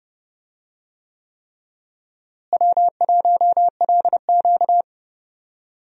Morse code, heard as a keyed tone, about 30 wpm, 700 Hz. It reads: W1LQ
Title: USA Call Signs - 4 Characters - Repeated 3x - 30wpm